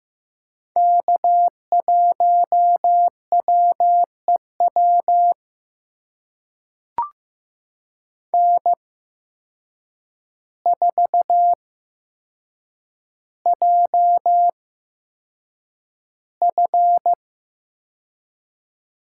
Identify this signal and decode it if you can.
Morse code, heard as a keyed tone, about 15 wpm, spaced as for 7 wpm, 700 Hz. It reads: K1WEW N4JF
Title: USA Call Signs - 5 Characters - 8x Character Spacing - 15wpm